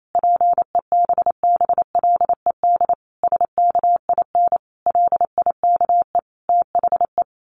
Morse code, heard as a keyed tone, about 28 wpm, 700 Hz. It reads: PEBBLED SKIN LIKE THE
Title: A Sound Of Thunder 28wpm